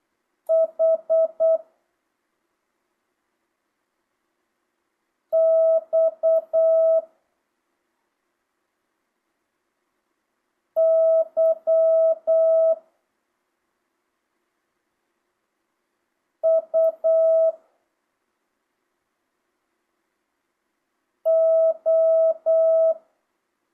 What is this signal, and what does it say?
Morse code, heard as a keyed tone, about 8 wpm, 650 Hz. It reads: H X Y U O
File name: jh8cbh / 2025cw